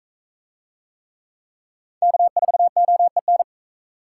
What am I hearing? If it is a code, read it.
Morse code, heard as a keyed tone, about 42 wpm, 700 Hz. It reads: K4OEN